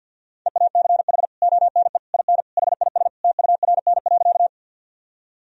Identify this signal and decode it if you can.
Morse code, heard as a keyed tone, about 51 wpm, 700 Hz. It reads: EACH ONE IN HIS TURN.
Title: Sentences from Top 1000 Words - Review Part 1 - 50wpm